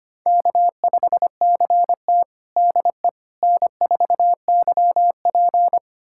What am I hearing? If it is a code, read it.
Morse code, heard as a keyed tone, about 25 wpm, 700 Hz. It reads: K5CT DE N4YP
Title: USA Callsign DE Callsign - 4 Characters - 25wpm